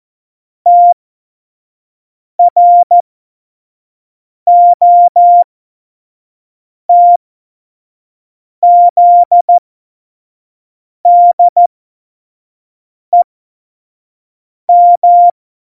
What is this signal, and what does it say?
Morse code, heard as a keyed tone, about 14 wpm, spaced as for 5 wpm, 700 Hz. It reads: TROTZDEM
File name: Morsekurs / Witze / Witz 22. Oktober 2019, 25 F70